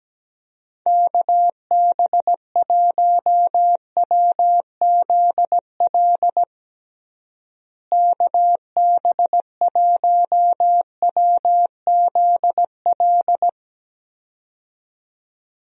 Morse code, heard as a keyed tone, about 17 wpm, 700 Hz. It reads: KB1WZL KB1WZL
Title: USA Call Signs - 6 Characters - Repeated 2x - 17wpm